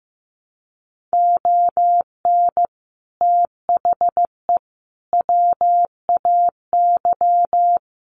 Morse code, heard as a keyed tone, about 15 wpm, 700 Hz. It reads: ON THE WAY
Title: Sets of 4 Words 15wpm